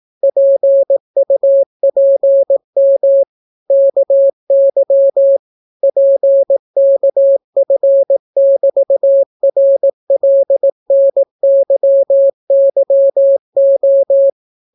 Morse code, heard as a keyed tone, about 18 wpm, 550 Hz. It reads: PUPM KY PKF<BT>RLNYYO